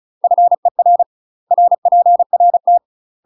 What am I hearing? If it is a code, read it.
Morse code, heard as a keyed tone, about 34 wpm, 700 Hz. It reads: FER RPRT
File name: UK MASTER QSOs TRAINING / 35WPM / UK MASTER QSOs 0031